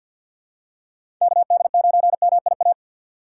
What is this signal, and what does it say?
Morse code, heard as a keyed tone, about 49 wpm, 700 Hz. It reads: KD9MIA
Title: USA Call Signs - 2x Character Spacing - 50wpm